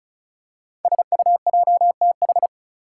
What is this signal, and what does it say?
Morse code, heard as a keyed tone, about 36 wpm, 700 Hz. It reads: SUJTH
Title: Code Groups - 5 Letter:Numbers - 35wpm